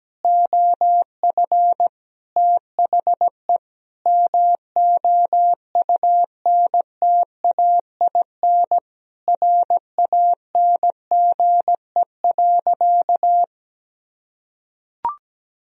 Morse code, heard as a keyed tone, about 17 wpm, 700 Hz. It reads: OF THE MOUNTAIN RANGE.
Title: Sentences from Top 1000 Words - Review Part 4 - 17wpm